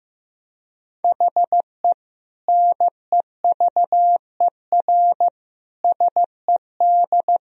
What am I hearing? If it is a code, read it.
Morse code, heard as a keyed tone, about 16 wpm, 700 Hz. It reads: HE NEVER SED
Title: Sentences from Top 500 Words and Abbreviations 15wpm